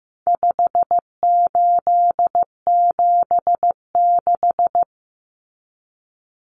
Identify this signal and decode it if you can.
Morse code, heard as a keyed tone, about 14 wpm, 700 Hz. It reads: 5876
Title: Numbers - 4-Digit 15wpm